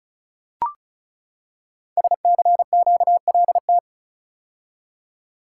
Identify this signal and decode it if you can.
Morse code, heard as a keyed tone, about 36 wpm, 700 Hz. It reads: SCQLT
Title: Code Groups - 5 Letters - 35wpm